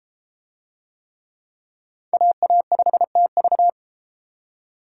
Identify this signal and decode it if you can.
Morse code, heard as a keyed tone, about 30 wpm, 700 Hz. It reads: AA5TV